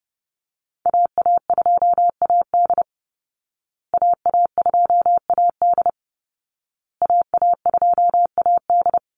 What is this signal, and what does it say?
Morse code, heard as a keyed tone, about 30 wpm, 700 Hz. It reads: AA2AD AA2AD AA2AD
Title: Call Signs Repeated 2x - 30wpm